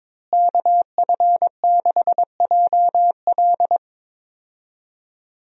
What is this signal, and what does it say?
Morse code, heard as a keyed tone, about 22 wpm, 700 Hz. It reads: KF6JL
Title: USA Call Signs - 5 Characters - Repeated 3x - 22wpm